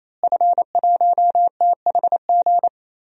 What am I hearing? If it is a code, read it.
Morse code, heard as a keyed tone, about 28 wpm, 700 Hz. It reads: F1THG
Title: DXCC-100 Call Signs - 5 Characters - Repeated 3x - 28wpm